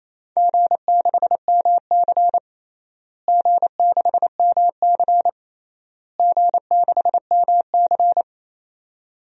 Morse code, heard as a keyed tone, about 27 wpm, 700 Hz. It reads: G6MC G6MC G6MC
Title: DXCC-100 Call Signs - 4 Characters - Repeated 3x - 28wpm